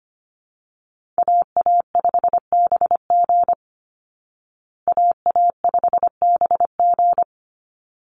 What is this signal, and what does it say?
Morse code, heard as a keyed tone, about 25 wpm, 700 Hz. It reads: AA5BG AA5BG
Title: Call Signs Repeated 3x - 25wpm